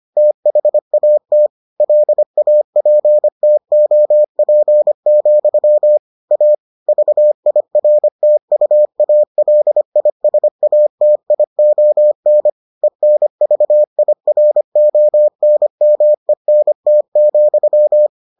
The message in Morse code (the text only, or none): THAT LAPTOP, A VIRTUALISATION ENVIRONMENT,